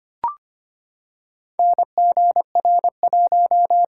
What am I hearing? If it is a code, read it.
Morse code, heard as a keyed tone, about 25 wpm, 690 Hz. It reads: NGR1
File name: practice / 25wpm / Four Letter-Number Combinations - 25wpm